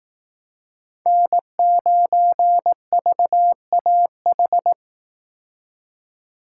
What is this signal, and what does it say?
Morse code, heard as a keyed tone, about 18 wpm, 700 Hz. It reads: N9VAH
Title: Speed Racing - USA Call Signs 15wpm